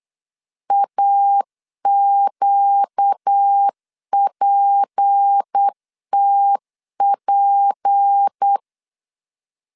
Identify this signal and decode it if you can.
Morse code, heard as a keyed tone, about 8 wpm, 800 Hz. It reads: AQPTP
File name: LEZIONI / 6 PKX0 / 6 P 8wpm 3025